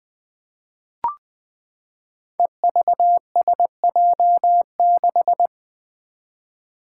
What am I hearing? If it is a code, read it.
Morse code, heard as a keyed tone, about 20 wpm, 700 Hz. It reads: EVSJ6
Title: Five Challenging Letter-Number Combinations 20wpm